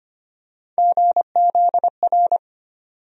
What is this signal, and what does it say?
Morse code, heard as a keyed tone, about 25 wpm, 700 Hz. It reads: GZR
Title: USA Call Sign Postfixes - 3 Letters 25wpm